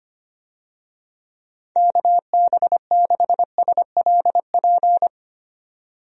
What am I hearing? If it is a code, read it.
Morse code, heard as a keyed tone, about 25 wpm, 700 Hz. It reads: KB6SLP
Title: USA Call Signs - 6 Characters - 25wpm